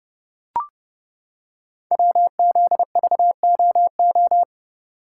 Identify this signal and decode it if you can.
Morse code, heard as a keyed tone, about 30 wpm, 700 Hz. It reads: WZVOO
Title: Code Groups - 5 Letter:Numbers - 30wpm